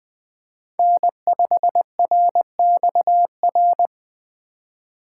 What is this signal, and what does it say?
Morse code, heard as a keyed tone, about 20 wpm, 700 Hz. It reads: N5RXR
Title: USA Call Signs - 5 Characters - 2x Character Spacing - 20wpm